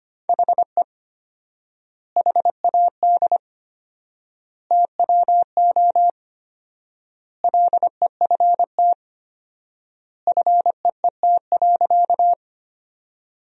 Morse code, heard as a keyed tone, about 25 wpm, 700 Hz. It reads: HE HAD TWO LEFT FEET.
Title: 4x Word-Spacing - Sentences from Top 300 Words - 25wpm